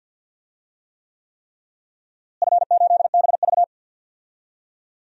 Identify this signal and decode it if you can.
Morse code, heard as a keyed tone, about 46 wpm, 700 Hz. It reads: F8BV
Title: DXCC-100 Call Signs - 4 Characters - Repeated 3x - 50wpm